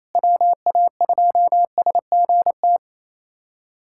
Morse code, heard as a keyed tone, about 28 wpm, 700 Hz. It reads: WA2SGT
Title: USA Callsign DE Callsign 28wpm